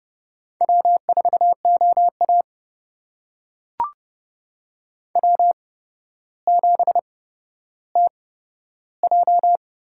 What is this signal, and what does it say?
Morse code, heard as a keyed tone, about 30 wpm, spaced as for 13 wpm, 700 Hz. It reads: W4OA W7TJ
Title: USA Call Signs - 4 Characters - 8x Character Spacing - 30wpm